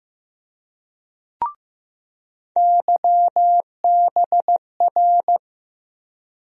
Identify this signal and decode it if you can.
Morse code, heard as a keyed tone, about 15 wpm, 700 Hz. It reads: YBR
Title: USA Call Sign Postfixes - 3 Letters 15wpm